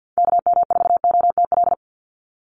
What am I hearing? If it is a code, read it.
Morse code, heard as a keyed tone, about 50 wpm, 700 Hz. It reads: KM4OTL
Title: USA Call Signs 50wpm